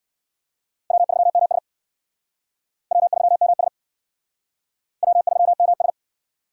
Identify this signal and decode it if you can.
Morse code, heard as a keyed tone, about 66 wpm, 700 Hz. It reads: W3MS W3MS W3MS